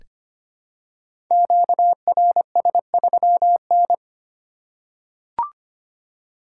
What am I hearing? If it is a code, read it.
Morse code, heard as a keyed tone, about 25 wpm, 700 Hz. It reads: QRS3N
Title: Code Groups - 5 Letter:Numbers - 25wpm